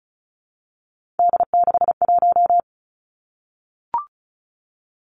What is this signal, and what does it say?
Morse code, heard as a keyed tone, about 36 wpm, 700 Hz. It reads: D61
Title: Three Challenging Letter-Number Combinations 35wpm